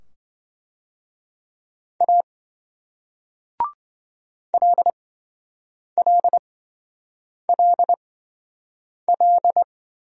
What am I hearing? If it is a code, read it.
Morse code, heard as a keyed tone, about 26 wpm, spaced as for 7 wpm, 700 Hz. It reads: A LLLL